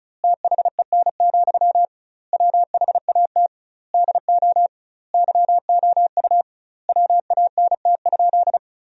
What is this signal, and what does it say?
Morse code, heard as a keyed tone, about 36 wpm, 700 Hz. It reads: THEN, WHAT DO YOU WANT?
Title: 2x Word-Spacing - Sentences from Top 200 Words - 35wpm